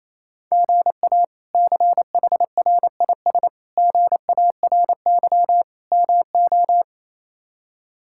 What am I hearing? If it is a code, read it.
Morse code, heard as a keyed tone, about 28 wpm, 700 Hz. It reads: GA CHRIS GARY MO
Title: K1USN SST QSO Practice - 28wpm